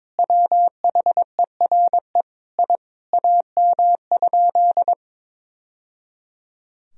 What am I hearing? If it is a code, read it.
Morse code, heard as a keyed tone, about 22 wpm, 700 Hz. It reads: WHERE I AM?
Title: Sentences from Top 1000 Words - Review Part 2 - 22wpm